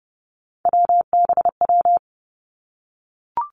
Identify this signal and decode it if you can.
Morse code, heard as a keyed tone, about 30 wpm, 700 Hz. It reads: WBW
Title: Three Challenging Letter-Number Combinations 30wpm